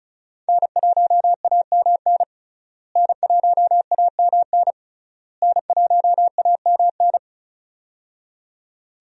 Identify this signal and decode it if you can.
Morse code, heard as a keyed tone, about 34 wpm, 700 Hz. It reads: N1AMN N1AMN N1AMN
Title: USA Call Signs - 5 Characters - Repeated 3x - 35wpm